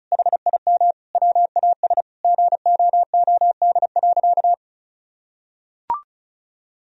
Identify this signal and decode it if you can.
Morse code, heard as a keyed tone, about 36 wpm, 700 Hz. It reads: HIM WAS GOOD.
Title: Sentences from Top 1000 Words - Review Part 4 - 35wpm